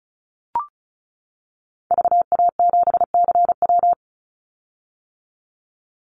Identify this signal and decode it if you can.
Morse code, heard as a keyed tone, about 36 wpm, 700 Hz. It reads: VA7CW
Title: Call Signs - 35wpm